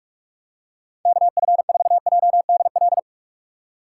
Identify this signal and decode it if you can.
Morse code, heard as a keyed tone, about 49 wpm, 700 Hz. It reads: KF4JDL